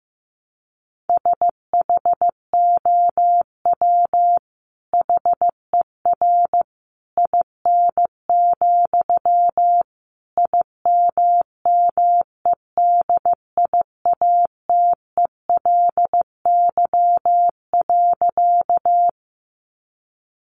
Morse code, heard as a keyed tone, about 14 wpm, 700 Hz. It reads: SHOW HER IN, IMMEDIATELY.
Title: Sentences from Top 1100 Words 15wpm